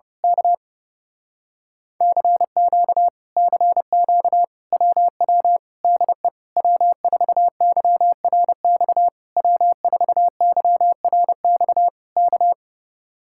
Morse code, heard as a keyed tone, about 28 wpm, 700 Hz. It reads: K CQ CQ WW DE W4YRX W4YRX K
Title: Speed-Racing Common QSO Phrases - 30wpm